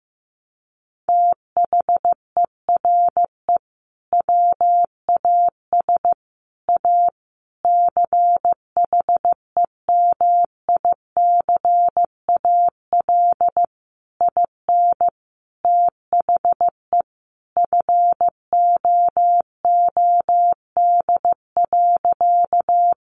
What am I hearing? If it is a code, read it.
Morse code, heard as a keyed tone, about 15 wpm, 700 Hz. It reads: THERE WAS A CHEMICAL IN THE FOOD.
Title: Sentences from Top 1200 Words 15wpm